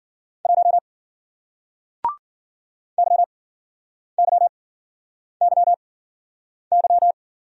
Morse code, heard as a keyed tone, about 51 wpm, spaced as for 9 wpm, 700 Hz. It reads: 1 YYYY